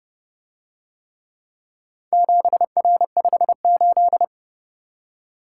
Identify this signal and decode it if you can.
Morse code, heard as a keyed tone, about 30 wpm, 700 Hz. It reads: 7R58